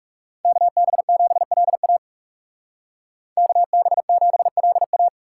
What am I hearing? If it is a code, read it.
Morse code, heard as a keyed tone, about 39 wpm, 700 Hz. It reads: KB7LA KB7LA